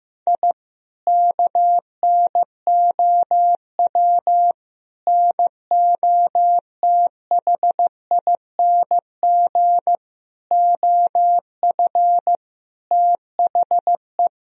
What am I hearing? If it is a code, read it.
Morse code, heard as a keyed tone, about 15 wpm, 700 Hz. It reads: I KNOW NOTHING OF THE
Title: Sentences from Top 1000 Words - Review Part 4 - 15wpm